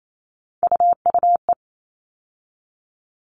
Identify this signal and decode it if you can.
Morse code, heard as a keyed tone, about 28 wpm, 700 Hz. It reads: UUE
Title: All Cut Numbers - 3-Digit 28wpm